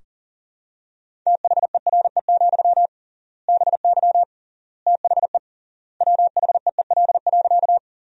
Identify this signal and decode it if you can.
Morse code, heard as a keyed tone, about 40 wpm, 700 Hz. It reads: THERE, BY THE WHEEL.
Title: 3x Word-Spacing - Sentences from Top 500 Words - 40wpm